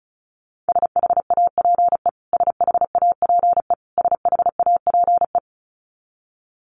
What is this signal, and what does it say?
Morse code, heard as a keyed tone, about 34 wpm, 700 Hz. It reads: SHAPE SHAPE SHAPE
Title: Top 500 Words Repeated 35wpm